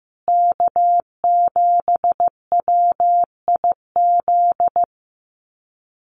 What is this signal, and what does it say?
Morse code, heard as a keyed tone, about 15 wpm, 700 Hz. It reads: K7WIZ
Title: USA Call Signs Repeated 2x 15wpm